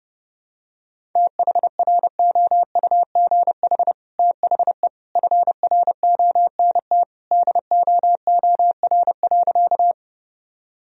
Morse code, heard as a keyed tone, about 30 wpm, 700 Hz. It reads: THROUGH THE FRONT DOOR.
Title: Sentences from Top 1000 Words - Review Part 2 - 30wpm